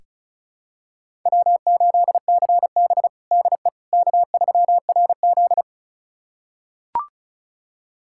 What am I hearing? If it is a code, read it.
Morse code, heard as a keyed tone, about 36 wpm, 700 Hz. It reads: W8CB DE K3RZ
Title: USA Callsign DE Callsign - 4 Characters - 35wpm